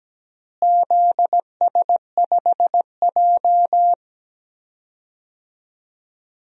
Morse code, heard as a keyed tone, about 17 wpm, 700 Hz. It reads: ZS5J
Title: DXCC-100 Call Signs - 4 Characters - Repeated 3x - 17wpm